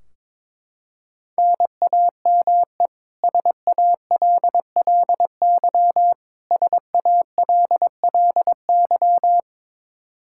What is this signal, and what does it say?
Morse code, heard as a keyed tone, about 22 wpm, 700 Hz. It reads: NAME SALLY SALLY
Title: K1USN SST QSO Practice 22wpm